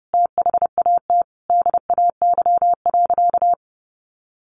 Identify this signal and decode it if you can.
Morse code, heard as a keyed tone, about 30 wpm, 700 Hz. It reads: THAT DAY.
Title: Sentences from Top 1500 Words 30wpm